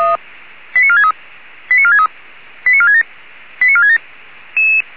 Five-tone selective call (ZVEI), heard as quick runs of five tones, tones of about 70 ms. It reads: 89463 89463 89467 89467